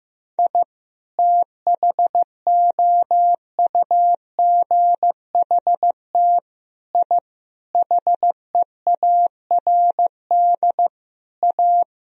Morse code, heard as a keyed tone, about 15 wpm, 700 Hz. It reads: I THOUGHT I HEARD A
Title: Sentences from Top 1000 Words - Review Part 1 - 15wpm